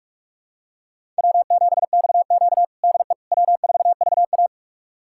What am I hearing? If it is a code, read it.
Morse code, heard as a keyed tone, about 46 wpm, 700 Hz. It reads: W7XQ DE W4VA
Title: USA Callsign DE Callsign - 4 Characters - 45wpm